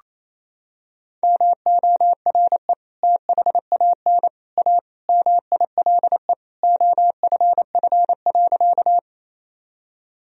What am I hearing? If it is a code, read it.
Morse code, heard as a keyed tone, about 28 wpm, 700 Hz. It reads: MORE THAN A MILE OFF.
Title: Sentences from Top 1000 Words - Review Part 1 - 28wpm